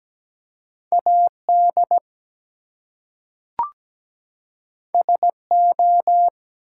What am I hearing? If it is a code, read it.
Morse code, heard as a keyed tone, about 17 wpm, 700 Hz. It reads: AD SO
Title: Two Letter Words - 17wpm